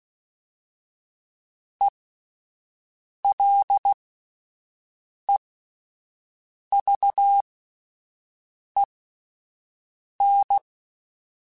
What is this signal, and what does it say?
Morse code, heard as a keyed tone, about 16 wpm, spaced as for 5 wpm, 800 Hz. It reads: ELEVEN